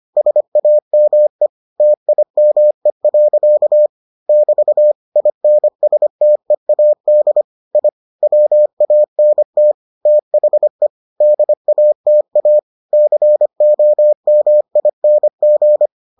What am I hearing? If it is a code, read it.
Morse code, heard as a keyed tone, about 25 wpm, 600 Hz. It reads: SAME TIME. <BT> INSTEAD I WANT THE DATA COMING